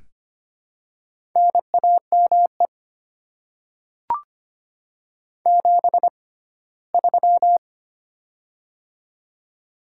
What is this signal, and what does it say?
Morse code, heard as a keyed tone, about 25 wpm, spaced as for 17 wpm, 700 Hz. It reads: NAME 73
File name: Top 20 QSO Elements - 6x Character Spacing - 25wpm